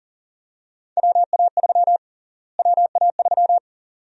Morse code, heard as a keyed tone, about 37 wpm, 700 Hz. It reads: WA3 WA3
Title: USA Call Sign Prefixes - Repeated 2x - 40wpm